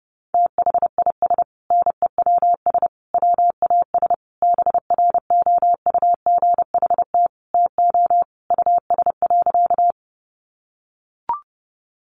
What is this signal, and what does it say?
Morse code, heard as a keyed tone, about 30 wpm, 700 Hz. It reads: THIS NEWS WAS BROUGHT TO US.
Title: Sentences from Top 800 Words 30wpm